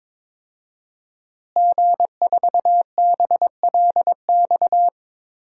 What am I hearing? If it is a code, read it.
Morse code, heard as a keyed tone, about 23 wpm, 700 Hz. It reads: G4BLX
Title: DXCC-100 Call Signs - 5 Characters - Repeated 2x - 22wpm